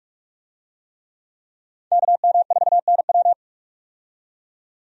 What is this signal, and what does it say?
Morse code, heard as a keyed tone, about 42 wpm, 700 Hz. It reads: KM4NW
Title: USA Call Signs - 5 Characters - 45wpm